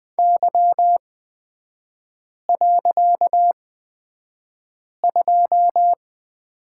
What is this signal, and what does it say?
Morse code, heard as a keyed tone, about 20 wpm, spaced as for 5 wpm, 700 Hz. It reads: Y.2
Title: 255 Lesson 41e - Period - Three Characters 20wpm